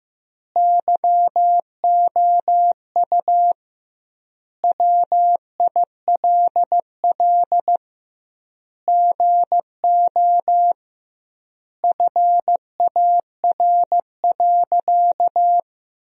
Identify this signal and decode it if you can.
Morse code, heard as a keyed tone, about 15 wpm, 700 Hz. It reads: YOU WILL GO FAR.
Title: 2x Word-Spacing - Sentences from Top 400 Words - 15wpm